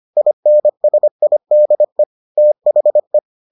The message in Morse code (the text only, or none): INSIDE THE